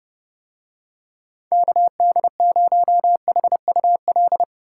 Morse code, heard as a keyed tone, about 30 wpm, 700 Hz. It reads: KD0HUL